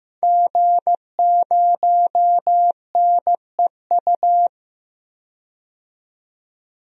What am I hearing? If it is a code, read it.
Morse code, heard as a keyed tone, about 15 wpm, 700 Hz. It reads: G0NEU
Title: DXCC-100 Call Signs - 5 Characters - 15wpm